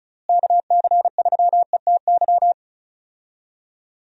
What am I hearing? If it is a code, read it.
Morse code, heard as a keyed tone, about 36 wpm, 700 Hz. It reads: KC3ETY
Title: USA Call Signs - 6 Characters - Repeated 2x - 35wpm